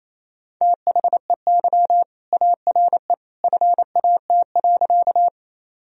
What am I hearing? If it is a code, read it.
Morse code, heard as a keyed tone, about 28 wpm, 700 Hz. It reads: THEY ARE FAT.
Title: Sentences from Top 1000 Words - Review Part 4 - 28wpm